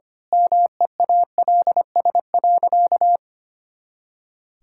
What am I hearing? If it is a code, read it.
Morse code, heard as a keyed tone, about 25 wpm, 700 Hz. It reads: MEALS.
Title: Pangram Challenge v2 - 25wpm